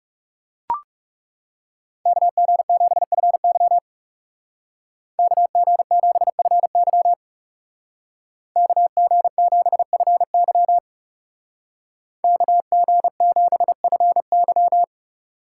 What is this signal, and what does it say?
Morse code, heard as a keyed tone, about 38 wpm, 700 Hz. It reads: KG7FY KG7FY KG7FY KG7FY
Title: Speed Racing - USA 2x2 Call Signs 30wpm